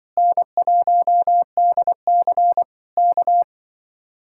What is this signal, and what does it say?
Morse code, heard as a keyed tone, about 24 wpm, 700 Hz. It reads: N1DC K
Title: Speed-Racing Common QSO Phrases - 20wpm